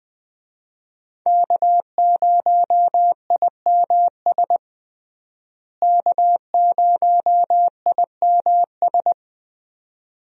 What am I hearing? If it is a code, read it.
Morse code, heard as a keyed tone, about 20 wpm, 700 Hz. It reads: K0IMS K0IMS
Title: USA Call Signs - 5 Characters - Repeated 2x - 20wpm